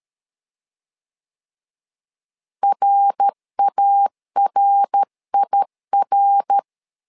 Morse code, heard as a keyed tone, about 13 wpm, 800 Hz. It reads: RARIR
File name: LEZIONI / 3 MWRO / 3 MWR 12wpm 1711